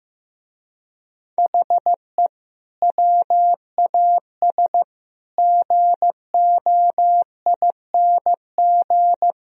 Morse code, heard as a keyed tone, about 15 wpm, 700 Hz. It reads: HE WAS GOING
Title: Speed-Racing - Sets of 4 Words 15wpm